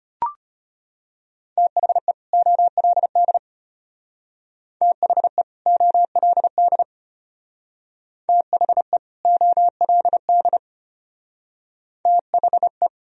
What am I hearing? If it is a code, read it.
Morse code, heard as a keyed tone, about 31 wpm, 700 Hz. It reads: THE OLD THE OLD THE OLD THE
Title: Speed-Racing - Sets of 2 Words 25wpm